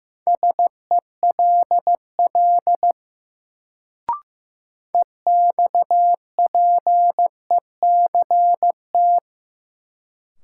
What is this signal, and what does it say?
Morse code, heard as a keyed tone, about 15 wpm, 700 Hz. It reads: SELL EXPECT
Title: Sentences from Top 1100 Words - New Word Drill Alt - 15wpm